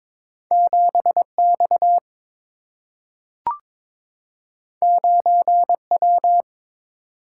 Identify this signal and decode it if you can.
Morse code, heard as a keyed tone, about 22 wpm, 700 Hz. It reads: 7X 9W
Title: Single-Letter Single-Number 22wpm